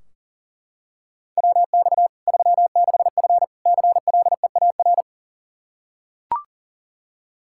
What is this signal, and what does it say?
Morse code, heard as a keyed tone, about 40 wpm, 700 Hz. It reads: WX 36F CLEAR
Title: Practice QSO - Exchange 2 - Info - 40wpm